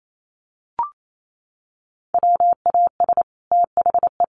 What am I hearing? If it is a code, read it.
Morse code, heard as a keyed tone, about 28 wpm, 700 Hz. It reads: WAS THE
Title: Sets of 2 Words 28wpm